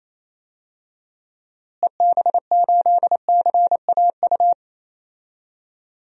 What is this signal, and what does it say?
Morse code, heard as a keyed tone, about 28 wpm, 700 Hz. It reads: EB8CAU
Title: DXCC-100 Call Signs - 6 Characters - Repeated 3x - 28wpm